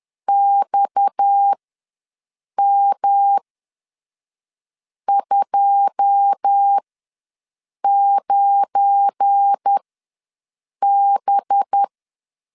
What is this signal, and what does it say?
Morse code, heard as a keyed tone, about 10 wpm, spaced as for 5 wpm, 800 Hz. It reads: XM29B
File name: LEZIONI / 9 2 ws 10wpm 4130